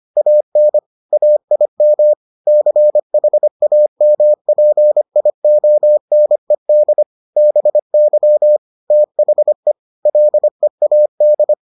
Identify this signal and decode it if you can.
Morse code, heard as a keyed tone, about 25 wpm, 600 Hz. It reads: AN AIM CHAMPIONED BY THE LEAD